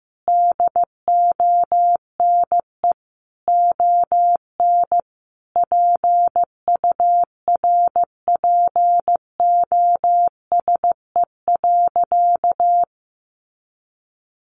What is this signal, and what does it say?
Morse code, heard as a keyed tone, about 15 wpm, 700 Hz. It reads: DONE ON PURPOSE.
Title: Sentences from Top 800 Words 15wpm